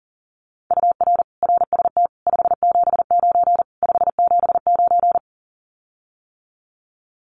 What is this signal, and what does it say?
Morse code, heard as a keyed tone, about 40 wpm, 700 Hz. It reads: UR RST 579 579
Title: Common QSO Phrases Encore 40wpm